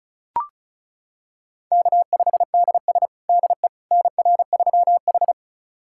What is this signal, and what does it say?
Morse code, heard as a keyed tone, about 34 wpm, 700 Hz. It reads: K5DS DE NR3H
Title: USA Callsign DE Callsign - 4 Characters - 35wpm